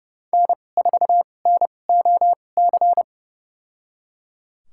Morse code, heard as a keyed tone, about 30 wpm, spaced as for 22 wpm, 700 Hz. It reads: N4NOC
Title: USA Call Signs - 5 Characters - 2x Character Spacing - 30wpm